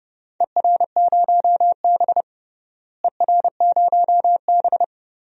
Morse code, heard as a keyed tone, about 30 wpm, 700 Hz. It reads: ER0B ER0B
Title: DXCC-100 Call Signs - 4 Characters - Repeated 3x - 30wpm